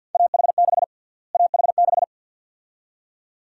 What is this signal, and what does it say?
Morse code, heard as a keyed tone, about 47 wpm, 700 Hz. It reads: AH6 AH6
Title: USA Call Sign Prefixes - Repeated 2x - 50wpm